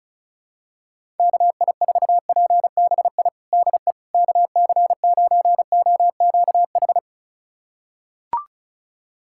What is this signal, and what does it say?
Morse code, heard as a keyed tone, about 36 wpm, 700 Hz. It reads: KI4PBI DE KC9OQH
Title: USA Callsign DE Callsign 35wpm